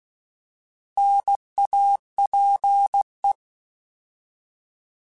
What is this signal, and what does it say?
Morse code, heard as a keyed tone, about 16 wpm, 800 Hz. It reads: NAPE
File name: Alphabetical / Alphabetical mp3s / Ns @15